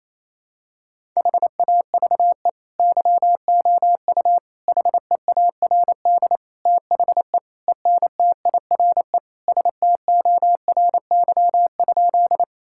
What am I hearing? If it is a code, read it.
Morse code, heard as a keyed tone, about 28 wpm, 700 Hz. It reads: HAVE YOU HEARD THE ENTIRE STORY?
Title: Sentences from Top 1000 Words - Review Part 5 - 28wpm